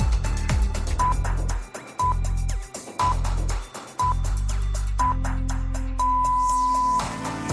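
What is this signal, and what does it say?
Radio time signal (time pips). It is heard from 1.0 to 7.0 s.
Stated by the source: Radio time signal from the Agropopular broadcast, Madrid.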